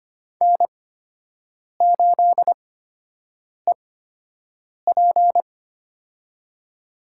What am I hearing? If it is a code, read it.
Morse code, heard as a keyed tone, about 25 wpm, spaced as for 7 wpm, 700 Hz. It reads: N8EP
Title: USA Call Signs - 8x Character Spacing - 25wpm